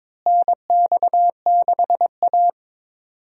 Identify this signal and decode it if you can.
Morse code, heard as a keyed tone, about 22 wpm, 700 Hz. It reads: NX6A
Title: USA Call Signs - 4 Characters - Repeated 3x - 22wpm